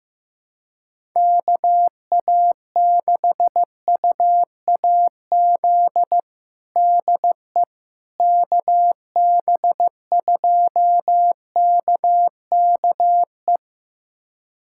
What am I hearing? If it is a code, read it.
Morse code, heard as a keyed tone, about 15 wpm, 700 Hz. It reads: KA6UAZ DE KB2KKE
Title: USA Callsign DE Callsign - 6 Characters - 15wpm